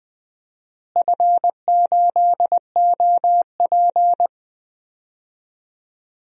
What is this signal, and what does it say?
Morse code, heard as a keyed tone, about 20 wpm, 700 Hz. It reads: F8OP
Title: DXCC-100 Call Signs - 4 Characters - Repeated 3x - 20wpm